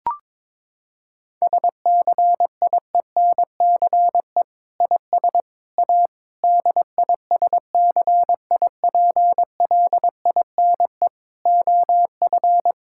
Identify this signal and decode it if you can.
Morse code, heard as a keyed tone, about 21 wpm, 700 Hz. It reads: SCIENCE IS A DISCIPLINE OF
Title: Sentences from Top 1000 Words - Review Part 6 - 22wpm